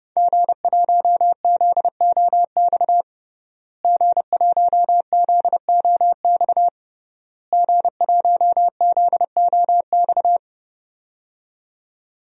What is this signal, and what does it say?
Morse code, heard as a keyed tone, about 32 wpm, 700 Hz. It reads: G1ZOX G1ZOX G1ZOX
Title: DXCC-100 Call Signs - 5 Characters - Repeated 3x - 30wpm